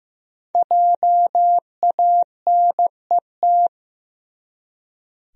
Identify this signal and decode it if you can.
Morse code, heard as a keyed tone, about 15 wpm, 700 Hz. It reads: JANET
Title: USA Common Names - Max 5 Letters - 15wpm